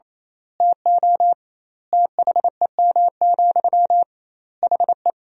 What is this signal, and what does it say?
Morse code, heard as a keyed tone, about 28 wpm, 700 Hz. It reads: TO THEM, HE
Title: 2x Word-Spacing - Sentences from Top 400 Words - 28wpm